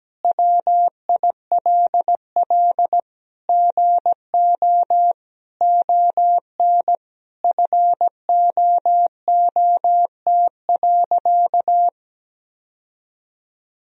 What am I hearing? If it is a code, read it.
Morse code, heard as a keyed tone, about 17 wpm, 700 Hz. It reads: WILL GO ON FOOT.
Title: Sentences from Top 1000 Words - Review Part 3 - 17wpm